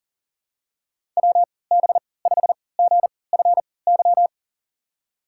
Morse code, heard as a keyed tone, about 40 wpm, spaced as for 23 wpm, 700 Hz. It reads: WB5GFY
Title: USA Call Signs - 3x Character Spacing - 40wpm